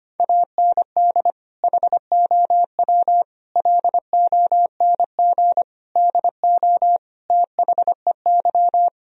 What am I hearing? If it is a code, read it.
Morse code, heard as a keyed tone, about 25 wpm, 700 Hz. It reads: AND HOW LONG DO THEY
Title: Sentences from Top 1000 Words - Review Part 4 - 25wpm